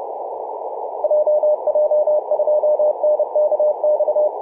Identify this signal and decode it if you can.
Morse code, heard as a keyed tone, about 30 wpm, 600 Hz, with radio noise. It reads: JJ2NKX
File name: VU4A 21.012MHz CW